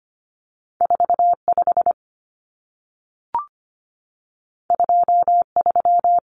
Morse code, heard as a keyed tone, about 25 wpm, 700 Hz. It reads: 45 23
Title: Numbers - 2-Digit 25wpm